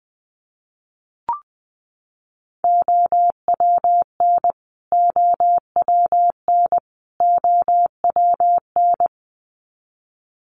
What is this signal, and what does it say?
Morse code, heard as a keyed tone, about 20 wpm, 700 Hz. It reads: OWN OWN OWN
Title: Top 300 Words Repeated 20wpm